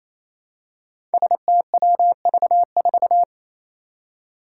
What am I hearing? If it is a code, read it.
Morse code, heard as a keyed tone, about 28 wpm, 700 Hz. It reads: STWV4
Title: Five Challenging Letter-Number Combinations 28wpm